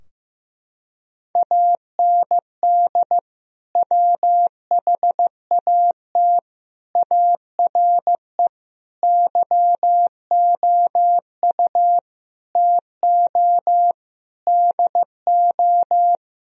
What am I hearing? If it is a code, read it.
Morse code, heard as a keyed tone, about 15 wpm, 700 Hz. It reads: AND WHAT ARE YOU TO DO
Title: Sentences from Top 1000 Words - Review Part 2 - 15wpm